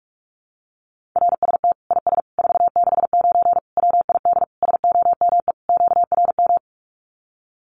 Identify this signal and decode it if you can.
Morse code, heard as a keyed tone, about 44 wpm, 700 Hz. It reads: RST IS 469 WID SOME QRM
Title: Common QSO Phrases 45wpm